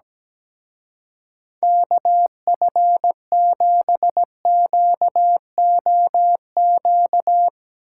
Morse code, heard as a keyed tone, about 17 wpm, 700 Hz. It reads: KF7QOQ
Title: Speed Racing - USA 2x3 Call Signs 17wpm